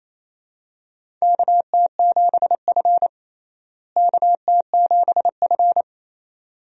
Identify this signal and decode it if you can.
Morse code, heard as a keyed tone, about 28 wpm, 700 Hz. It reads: KT7F KT7F
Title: USA Call Signs - 4 Characters - Repeated 2x - 28wpm